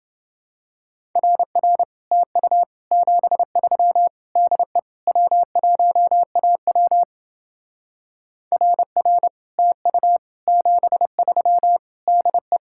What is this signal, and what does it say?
Morse code, heard as a keyed tone, about 29 wpm, 700 Hz. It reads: RR TU 73 DE W1AW RR TU 73 DE